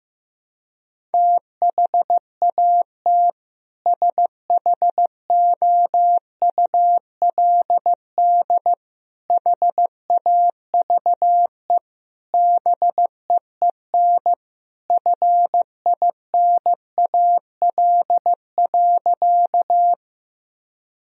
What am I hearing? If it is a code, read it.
Morse code, heard as a keyed tone, about 16 wpm, 700 Hz. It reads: THAT SHOULD HAVE BEEN FINAL.
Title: Sentences from Top 1000 Words - Review Part 2 - 15wpm